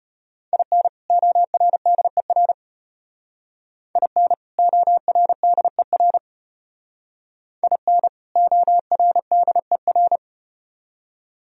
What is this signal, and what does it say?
Morse code, heard as a keyed tone, about 34 wpm, 700 Hz. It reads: IN ORDER IN ORDER IN ORDER